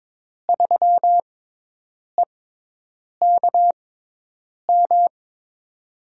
Morse code, heard as a keyed tone, about 22 wpm, spaced as for 8 wpm, 700 Hz. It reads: 3EKM
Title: USA Call Signs - 6 Characters - 6x Character Spacing - 22wpm